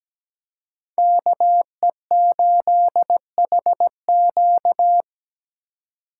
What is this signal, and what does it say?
Morse code, heard as a keyed tone, about 21 wpm, 700 Hz. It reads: KE8HQ
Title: Speed Racing - USA Call Signs 17wpm